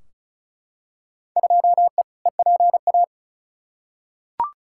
Morse code, heard as a keyed tone, about 34 wpm, 700 Hz. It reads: 2E EPA
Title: Field Day Exchanges Encore - 35wpm